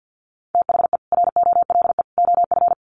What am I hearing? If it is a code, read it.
Morse code, heard as a keyed tone, about 50 wpm, 700 Hz. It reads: THE ROLE OF
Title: Sets of 3 Words 50wpm